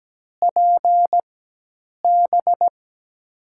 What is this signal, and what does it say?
Morse code, heard as a keyed tone, about 17 wpm, spaced as for 8 wpm, 700 Hz. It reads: PB
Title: USA Call Signs - 6 Characters - 4x Character Spacing - 17wpm